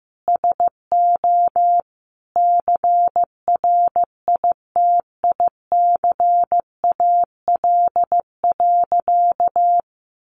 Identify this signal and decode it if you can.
Morse code, heard as a keyed tone, about 14 wpm, 700 Hz. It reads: SO CRITICAL.
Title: Sentences from Top 1100 Words 15wpm